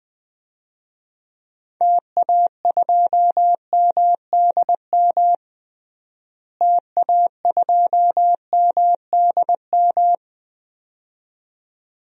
Morse code, heard as a keyed tone, about 21 wpm, 690 Hz. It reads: TA2MDM TA2MDM
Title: DXCC-100 Call Signs - 6 Characters - Repeated 2x - 20wpm